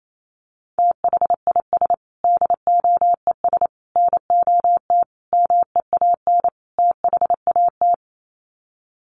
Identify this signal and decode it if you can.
Morse code, heard as a keyed tone, about 28 wpm, 700 Hz. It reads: THIS DOES NOT MEAN THAT
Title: Sets of 5 Words Encore2 28wpm